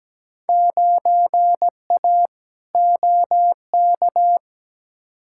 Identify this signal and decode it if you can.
Morse code, heard as a keyed tone, about 17 wpm, 700 Hz. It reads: 9A OK
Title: Field Day Exchanges Encore - 17wpm